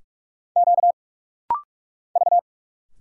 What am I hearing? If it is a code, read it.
Morse code, heard as a keyed tone, about 46 wpm, 690 Hz. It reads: Q V
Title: Single Letters - Rapid-Fire 45wpm